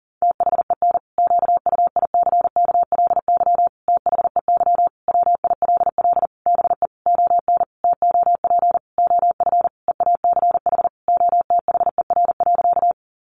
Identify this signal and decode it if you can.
Morse code, heard as a keyed tone, about 40 wpm, 700 Hz. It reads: THEN QUICKLY THEY WILL BE ON TOP OF EACH OTHER.
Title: Sentences from Top 700 Words 40wpm